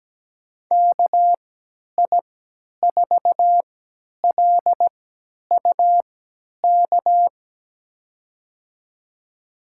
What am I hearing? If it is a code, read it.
Morse code, heard as a keyed tone, about 17 wpm, spaced as for 10 wpm, 700 Hz. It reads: KI4LUK
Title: USA Call Signs - 6 Characters - 3x Character Spacing - 17wpm